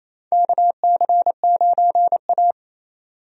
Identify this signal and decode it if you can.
Morse code, heard as a keyed tone, about 28 wpm, 700 Hz. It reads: KC9A
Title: USA Call Signs - 4 Characters - 28wpm